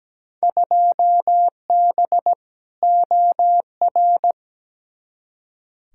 Morse code, heard as a keyed tone, about 17 wpm, 700 Hz. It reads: 2B OR
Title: Field Day Exchanges Encore - 17wpm